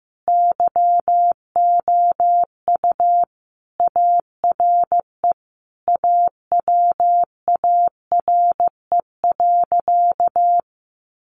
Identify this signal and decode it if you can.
Morse code, heard as a keyed tone, about 15 wpm, 700 Hz. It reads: YOU ARE AWARE.
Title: Sentences from Top 1300 Words 15wpm